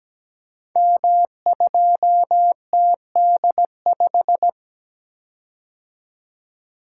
Morse code, heard as a keyed tone, about 17 wpm, 700 Hz. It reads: M2TD5
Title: Code Groups - 5 Letter:Numbers - 17wpm